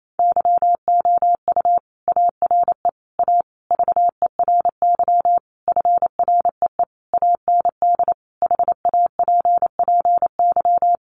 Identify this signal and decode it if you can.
Morse code, heard as a keyed tone, about 28 wpm, 700 Hz. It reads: YOU ARE A VERY FREE AND HAPPY
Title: Sentences from Top 1000 Words 28wpm